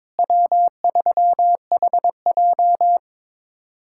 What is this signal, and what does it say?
Morse code, heard as a keyed tone, about 22 wpm, 700 Hz. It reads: W3HJ
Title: USA Callsign DE Callsign - 4 Characters - 22wpm